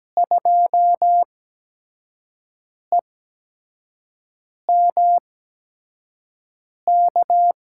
Morse code, heard as a keyed tone, about 17 wpm, spaced as for 5 wpm, 700 Hz. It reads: 2EMK
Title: USA Call Signs - 6 Characters - 8x Character Spacing - 17wpm